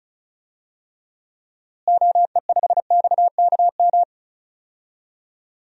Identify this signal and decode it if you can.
Morse code, heard as a keyed tone, about 36 wpm, 700 Hz. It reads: OE5XKM
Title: DXCC-100 Call Signs - 6 Characters - Repeated 3x - 35wpm